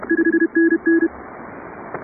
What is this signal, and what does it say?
Morse code, heard as a keyed tone, about 31 wpm, 340 Hz. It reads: 5NN
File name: chatham / 0216-15CW